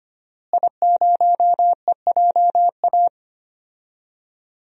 Morse code, heard as a keyed tone, about 25 wpm, 700 Hz. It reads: I0EJA
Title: DXCC-100 Call Signs - 5 Characters - Repeated 3x - 25wpm